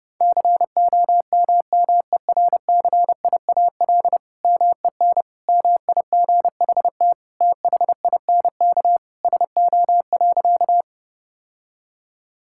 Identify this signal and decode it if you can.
Morse code, heard as a keyed tone, about 30 wpm, 700 Hz. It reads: COMMERCIAL MEN MIGHT THINK SO.
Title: Sentences from Top 1000 Words - Review Part 5 - 30wpm